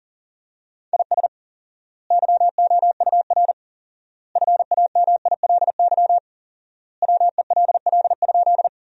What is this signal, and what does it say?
Morse code, heard as a keyed tone, about 40 wpm, 700 Hz. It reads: IS YOUR FAMILY WELL?
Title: 4x Word-Spacing - Sentences from Top 400 Words - 40wpm